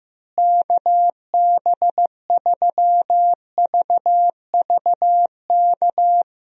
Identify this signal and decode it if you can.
Morse code, heard as a keyed tone, about 15 wpm, 700 Hz. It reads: KB3VVK
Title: USA Call Signs - 6 Characters - Repeated 3x - 15wpm